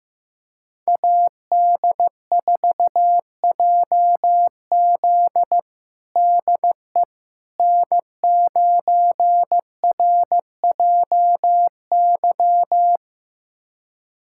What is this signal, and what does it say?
Morse code, heard as a keyed tone, about 15 wpm, 700 Hz. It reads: AD4JZ DE N9RJY
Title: USA Callsign DE Callsign - 5 Characters - 15wpm